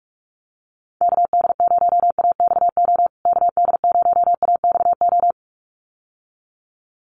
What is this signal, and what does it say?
Morse code, heard as a keyed tone, about 44 wpm, 700 Hz. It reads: KD0AXO KD0AXO
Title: USA Call Signs Repeated 2x 45wpm